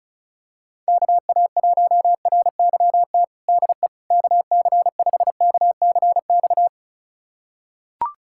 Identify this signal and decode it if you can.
Morse code, heard as a keyed tone, about 36 wpm, 700 Hz. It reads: KA1RYT DE KC5KCX
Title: USA Callsign DE Callsign - 6 Characters - 35wpm